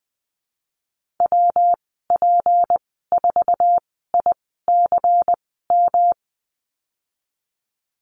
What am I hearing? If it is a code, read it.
Morse code, heard as a keyed tone, about 19 wpm, spaced as for 14 wpm, 700 Hz. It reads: WP4ICM
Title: 154 Lesson 25g - Letter p - Call Signs 20wpm